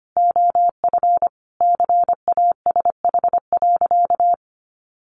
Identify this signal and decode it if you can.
Morse code, heard as a keyed tone, about 25 wpm, 700 Hz. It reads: OF CASH.
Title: Sentences from Top 1300 Words 25wpm